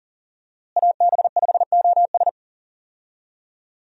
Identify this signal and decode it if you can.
Morse code, heard as a keyed tone, about 40 wpm, 700 Hz. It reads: AB5OS
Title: USA Call Signs - 5 Characters - Repeated 2x - 40wpm